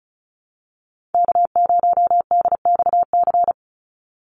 Morse code, heard as a keyed tone, about 34 wpm, 700 Hz. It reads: K0DXC
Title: Call Signs Repeated 3x - 35wpm